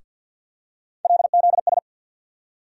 Morse code, heard as a keyed tone, about 47 wpm, 700 Hz. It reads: LZS
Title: USA Call Sign Postfixes - 3 Letters 50wpm